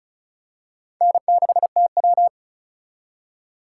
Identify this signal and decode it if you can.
Morse code, heard as a keyed tone, about 36 wpm, 700 Hz. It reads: N6TW
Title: K1USN SST QSO Practice - 35wpm